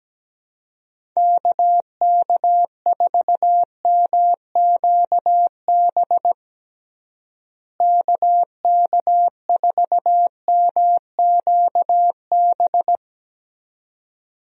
Morse code, heard as a keyed tone, about 17 wpm, 700 Hz. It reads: KK4MQB KK4MQB
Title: USA Call Signs - 6 Characters - Repeated 2x - 17wpm